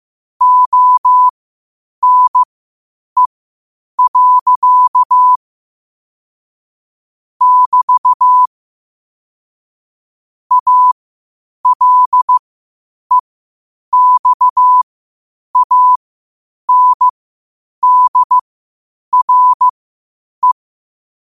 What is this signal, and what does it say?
Morse code, heard as a keyed tone, about 15 wpm, spaced as for 8 wpm, 1000 Hz. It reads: ONE. <BT> ALEXANDRE
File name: qotdfiles / 8wpm